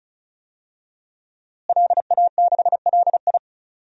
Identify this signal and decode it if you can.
Morse code, heard as a keyed tone, about 36 wpm, 700 Hz. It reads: LA6LI
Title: DXCC-100 Call Signs - 5 Characters - Repeated 2x - 35wpm